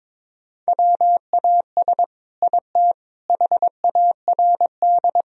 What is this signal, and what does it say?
Morse code, heard as a keyed tone, about 22 wpm, 700 Hz. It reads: WAS IT HARD
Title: Sentences from Top 1000 Words - Review Part 4 - 22wpm